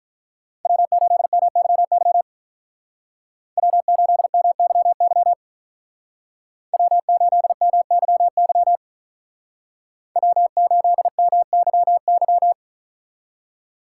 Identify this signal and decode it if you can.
Morse code, heard as a keyed tone, about 43 wpm, 700 Hz. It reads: W8MYY W8MYY W8MYY W8MYY
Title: Speed Racing - USA 1x3 Call Signs 35wpm